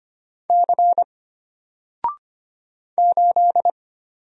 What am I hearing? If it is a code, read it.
Morse code, heard as a keyed tone, about 25 wpm, 700 Hz. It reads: C 8
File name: Single-Letter-Number-25wpm